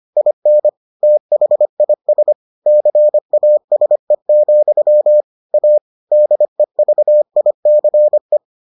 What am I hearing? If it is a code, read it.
Morse code, heard as a keyed tone, about 25 wpm, 600 Hz. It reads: IN THIS CASE, A DEVICE